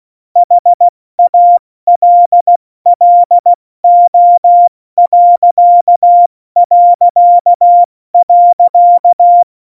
Morse code, heard as a keyed tone, about 16 wpm, 700 Hz. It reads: HALLO...
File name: Morsekurs / Witze / Witz 15. August 17, Tempo 70 F80